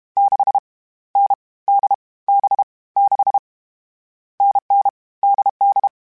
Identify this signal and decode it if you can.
Morse code, heard as a keyed tone, about 32 wpm, 800 Hz. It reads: 6 N D B 6 NN DD
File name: Difficult Combos / Difficult Combos-mp3s / NDB NDB6 @30